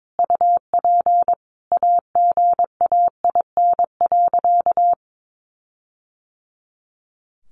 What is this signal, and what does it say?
Morse code, heard as a keyed tone, about 22 wpm, 700 Hz. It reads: UP AGAIN.
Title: Sentences from Top 1400 Words 22wpm